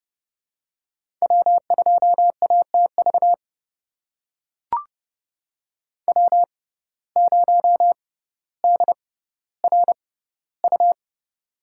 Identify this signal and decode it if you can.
Morse code, heard as a keyed tone, about 31 wpm, spaced as for 16 wpm, 700 Hz. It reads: W2ATV W0DRU
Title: USA Call Signs - 5 Characters - 6x Character Spacing - 30wpm